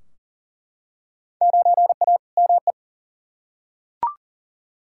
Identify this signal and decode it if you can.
Morse code, heard as a keyed tone, about 39 wpm, 700 Hz. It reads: 9A ME
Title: Field Day Exchanges Encore - 40wpm